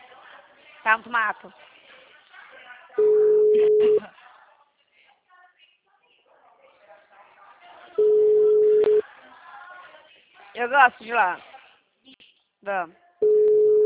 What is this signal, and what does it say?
Ringback tone (Europe, 425 Hz), with speech.